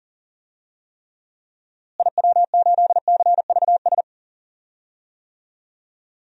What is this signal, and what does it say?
Morse code, heard as a keyed tone, about 40 wpm, 700 Hz. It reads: IW8CVS
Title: DXCC-100 Call Signs - 6 Characters - Repeated 2x - 40wpm